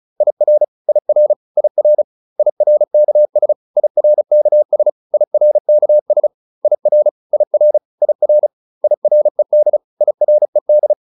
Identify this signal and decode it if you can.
Morse code, heard as a keyed tone, about 34 wpm, 600 Hz. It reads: IR IR IR IRKS IRKS IRKS IR IR IR IRED IRED